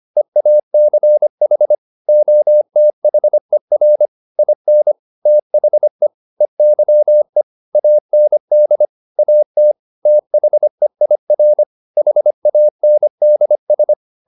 Morse code, heard as a keyed tone, about 25 wpm, 600 Hz. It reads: EACH OTHER IN THE EYE AND AT THEIR HANDS